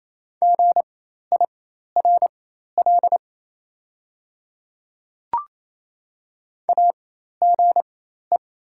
Morse code, heard as a keyed tone, about 27 wpm, spaced as for 13 wpm, 690 Hz. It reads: GIRL AGE
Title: 4x Character Spacing - Top 500 Words - 28wpm